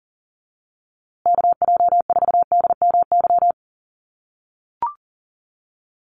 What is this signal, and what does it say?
Morse code, heard as a keyed tone, about 40 wpm, 700 Hz. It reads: KJ4DMY
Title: USA Call Signs Repeated 2x 40wpm